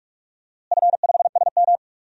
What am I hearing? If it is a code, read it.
Morse code, heard as a keyed tone, about 42 wpm, 700 Hz. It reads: F5SM
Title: DXCC-100 Call Signs - 4 Characters - Repeated 2x - 45wpm